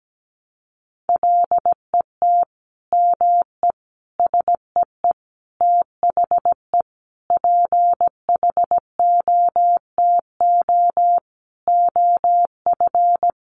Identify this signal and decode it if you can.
Morse code, heard as a keyed tone, about 17 wpm, 700 Hz. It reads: LET ME SEE THE PHOTO OF
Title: Sentences from Top 1200 Words 17wpm